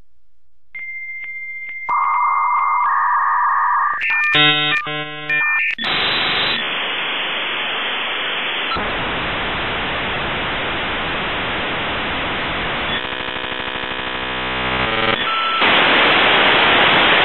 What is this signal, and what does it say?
Dial-up modem handshake.